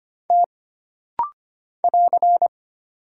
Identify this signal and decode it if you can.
Morse code, heard as a keyed tone, about 25 wpm, 700 Hz. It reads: T <AR>
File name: Single Character (Letter:Number:Punctuation:Prosign) - Rapid-Fire - 25wpm